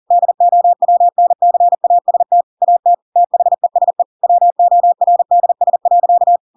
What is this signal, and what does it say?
Morse code, heard as a keyed tone, about 40 wpm, 700 Hz. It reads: DOWNCAST AT THESE WORDS.
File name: Pinocchio / Pinocchio ch8 40 40 ik1hge